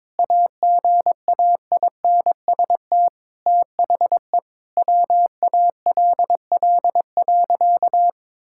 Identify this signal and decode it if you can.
Morse code, heard as a keyed tone, about 21 wpm, 700 Hz. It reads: AGAINST THE WALL.
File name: practice / 22wpm / Sentences with 2-Word Phrases - 22wpm